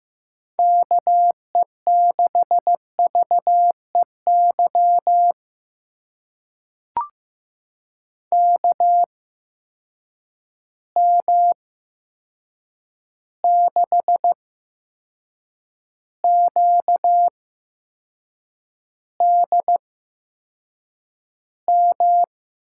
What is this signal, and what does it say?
Morse code, heard as a keyed tone, about 15 wpm, spaced as for 6 wpm, 700 Hz. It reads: KE6VEY KM6QDM
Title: USA Call Signs - 6 Characters - 8x Character Spacing - 15wpm